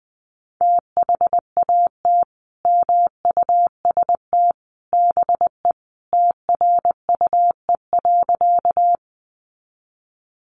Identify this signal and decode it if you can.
Morse code, heard as a keyed tone, about 20 wpm, 700 Hz. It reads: THAT MUST BE TRUE.
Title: Sentences from Top 400 Words 20wpm